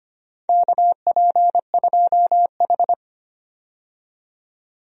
Morse code, heard as a keyed tone, about 25 wpm, 700 Hz. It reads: KP2H
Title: DXCC-100 Call Signs - 4 Characters - 25wpm